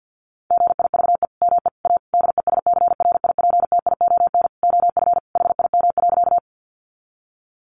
Morse code, heard as a keyed tone, about 50 wpm, 700 Hz. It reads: GIVE ME A DESCRIPTION OF HIM.
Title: Sentences from Top 1300 Words 50wpm